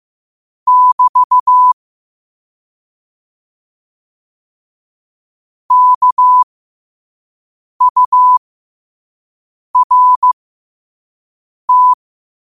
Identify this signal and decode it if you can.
Morse code, heard as a keyed tone, about 15 wpm, spaced as for 5 wpm, 1000 Hz. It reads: <BT> KURT
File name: qotdfiles / 5wpm